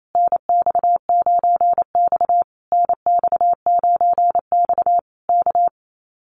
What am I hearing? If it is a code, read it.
Morse code, heard as a keyed tone, about 28 wpm, 700 Hz. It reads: NX9X NX9X K
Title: Common QSO Phrases Encore 28wpm